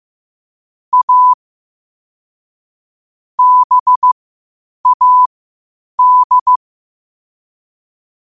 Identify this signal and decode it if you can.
Morse code, heard as a keyed tone, about 15 wpm, spaced as for 8 wpm, 1000 Hz. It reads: A BAD
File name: qotdfiles / 8wpm